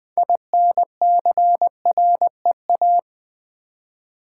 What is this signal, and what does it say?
Morse code, heard as a keyed tone, about 20 wpm, 700 Hz. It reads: INCREA
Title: Build A Word - 6 Letters At A Time - 12-Letter Words - 20wpm